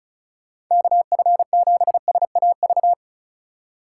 Morse code, heard as a keyed tone, about 45 wpm, 700 Hz. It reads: KF7SAV